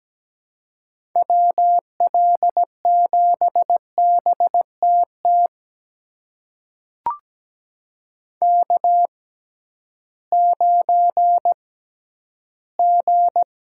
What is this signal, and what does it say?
Morse code, heard as a keyed tone, about 17 wpm, spaced as for 11 wpm, 700 Hz. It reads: WL7BTT K9G
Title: USA Call Signs - 6x Character Spacing - 17wpm